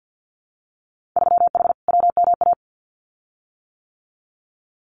Morse code, heard as a keyed tone, about 50 wpm, 700 Hz. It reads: 3H WMA
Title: Winter Field Day Exchanges Encore 50wpm